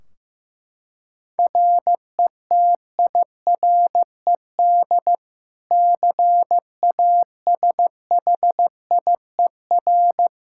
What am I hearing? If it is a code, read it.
Morse code, heard as a keyed tone, about 15 wpm, 700 Hz. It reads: RETIRED CASHIER
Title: Practice QSO - Exchange 3 - More Info - 15wpm